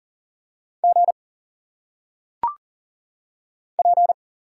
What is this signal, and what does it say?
Morse code, heard as a keyed tone, about 40 wpm, 700 Hz. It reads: G P
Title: Single Letters - Phonetic Pronunciation 40wpm